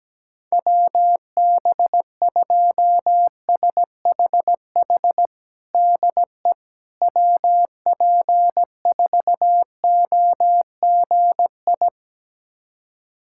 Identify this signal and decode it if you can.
Morse code, heard as a keyed tone, about 17 wpm, 700 Hz. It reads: WB2SHH DE WP4OGI
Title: USA Callsign DE Callsign - 6 Characters - 17wpm